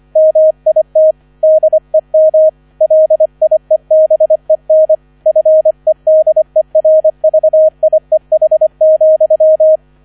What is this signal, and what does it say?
Morse code, heard as a keyed tone, about 25 wpm, 630 Hz. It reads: MIT DEM LIEBEN FEDERVIEH,